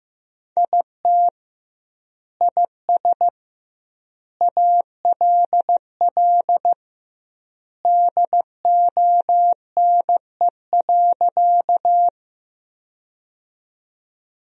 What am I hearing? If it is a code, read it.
Morse code, heard as a keyed tone, about 15 wpm, 700 Hz. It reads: IT IS ALL DONE.
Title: 2x Word-Spacing - Sentences from Top 500 Words - 15wpm